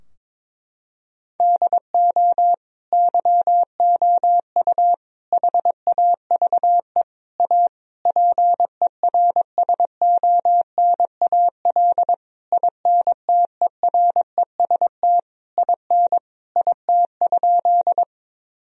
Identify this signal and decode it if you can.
Morse code, heard as a keyed tone, about 21 wpm, 700 Hz. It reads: DO YOU HAVE A PERSONAL INTEREST IN IT?
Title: Sentences from Top 1000 Words - Review Part 4 - 22wpm